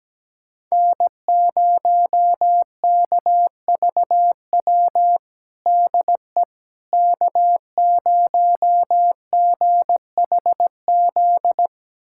Morse code, heard as a keyed tone, about 17 wpm, 700 Hz. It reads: N0KVW DE K0GHZ
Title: USA Callsign DE Callsign - 5 Characters - 17wpm